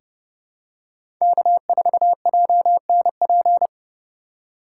Morse code, heard as a keyed tone, about 32 wpm, 700 Hz. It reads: K4JNP
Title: USA Call Signs - 2x Character Spacing - 30wpm